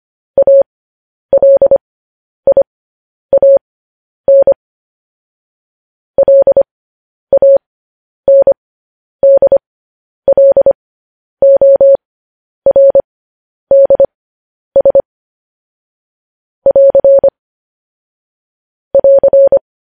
Morse code, heard as a keyed tone, about 25 wpm, spaced as for 10 wpm, 550 Hz. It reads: ALIAN LANDLORDS <AR> <AR>
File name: podcast / News headlines-1x-25x10